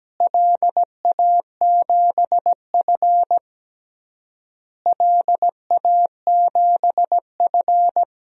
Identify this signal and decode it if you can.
Morse code, heard as a keyed tone, about 17 wpm, 700 Hz. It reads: LA7F LA7F
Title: DXCC-100 Call Signs - 4 Characters - Repeated 3x - 17wpm